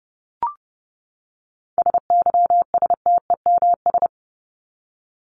Morse code, heard as a keyed tone, about 30 wpm, 700 Hz. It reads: SYSTEMS
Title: ES:S Suffix - Top 100 Words with Base Words 30wpm